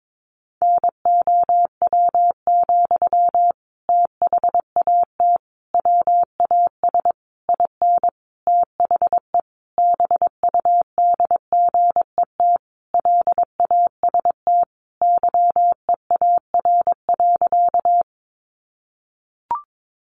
Morse code, heard as a keyed tone, about 22 wpm, 700 Hz. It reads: NOW, THAT WAS IN THE BUDGET LAST YEAR.
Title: Sentences from Top 1200 Words 22wpm